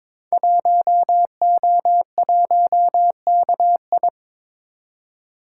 Morse code, heard as a keyed tone, about 22 wpm, 690 Hz. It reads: 1O1KI
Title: Code Groups - 5 Letter:Numbers - 22wpm